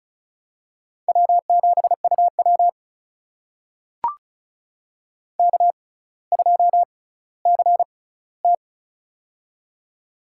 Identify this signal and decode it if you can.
Morse code, heard as a keyed tone, about 36 wpm, spaced as for 18 wpm, 700 Hz. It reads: W7UW K2CT
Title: USA Call Signs - 4 Characters - 6x Character Spacing - 35wpm